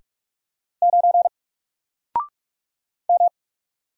Morse code, heard as a keyed tone, about 44 wpm, 700 Hz. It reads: 9 M
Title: Single Letter-Number 45wpm